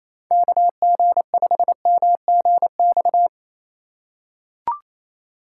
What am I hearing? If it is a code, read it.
Morse code, heard as a keyed tone, about 28 wpm, 700 Hz. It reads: KG5MGX
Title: USA Call Signs - 6 Characters - 3x Character Spacing - 28wpm